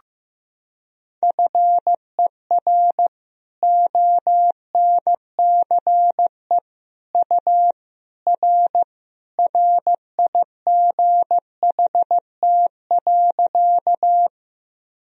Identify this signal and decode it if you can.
Morse code, heard as a keyed tone, about 15 wpm, 700 Hz. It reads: FER ONCE U R RIGHT.
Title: Sentences from Top 300 Words and Abbreviations 15wpm